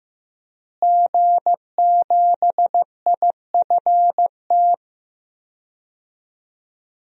Morse code, heard as a keyed tone, about 15 wpm, 700 Hz. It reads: G7IFT
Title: DXCC-100 Call Signs - 5 Characters - Repeated 3x - 15wpm